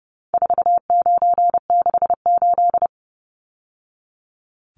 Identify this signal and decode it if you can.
Morse code, heard as a keyed tone, about 30 wpm, 700 Hz. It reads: 4968
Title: Numbers - 4-Digit 30wpm